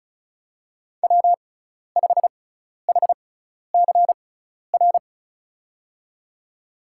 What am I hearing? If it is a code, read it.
Morse code, heard as a keyed tone, about 36 wpm, spaced as for 12 wpm, 700 Hz. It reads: W5HCR
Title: USA Call Signs - 6x Character Spacing - 35wpm